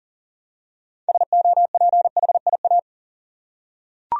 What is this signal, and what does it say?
Morse code, heard as a keyed tone, about 40 wpm, 700 Hz. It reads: SOPHIA
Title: USA Common Names - 40wpm